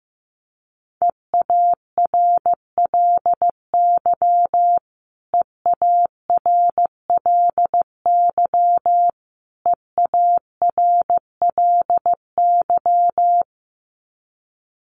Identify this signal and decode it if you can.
Morse code, heard as a keyed tone, about 15 wpm, 700 Hz. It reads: EARLY EARLY EARLY
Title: Top 500 Words Repeated 15wpm